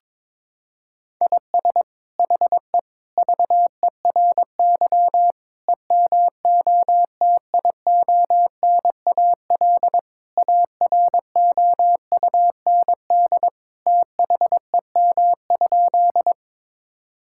Morse code, heard as a keyed tone, about 22 wpm, 700 Hz. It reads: IS HE VERY EMOTIONAL AROUND THEM?
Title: Sentences from Top 1000 Words - Review Part 5 - 22wpm